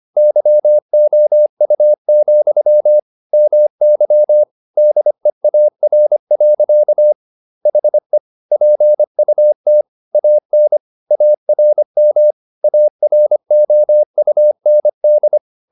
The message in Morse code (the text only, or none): YOU, MY DEAR. HE PUT AN ARM AROUND